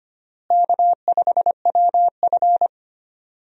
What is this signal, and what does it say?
Morse code, heard as a keyed tone, about 25 wpm, 700 Hz. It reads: K5WF
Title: USA Call Signs - 4 Characters - Repeated 3x - 25wpm